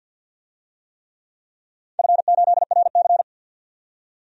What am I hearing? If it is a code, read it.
Morse code, heard as a keyed tone, about 51 wpm, 700 Hz. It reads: F8RC
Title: DXCC-100 Call Signs - 4 Characters - 50wpm